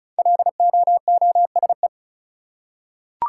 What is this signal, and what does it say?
Morse code, heard as a keyed tone, about 36 wpm, 700 Hz. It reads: LOOSE
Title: Five Letter Words - 35wpm